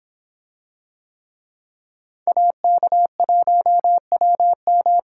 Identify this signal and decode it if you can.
Morse code, heard as a keyed tone, about 26 wpm, 700 Hz. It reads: AK1WM